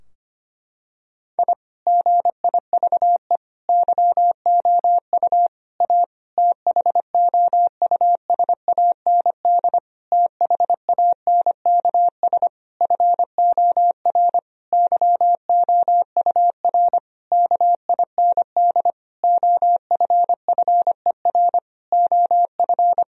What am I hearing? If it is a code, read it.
Morse code, heard as a keyed tone, about 25 wpm, 700 Hz. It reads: I GIVE YOU A THOUSAND THANKS FOR YOUR KIND OFFER OF
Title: Sentences from Top 1000 Words - Review Part 4 - 25wpm